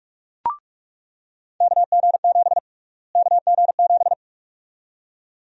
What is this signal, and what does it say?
Morse code, heard as a keyed tone, about 46 wpm, 700 Hz. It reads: KG7 KG7
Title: USA Call Sign Prefixes - Repeated 2x - 45wpm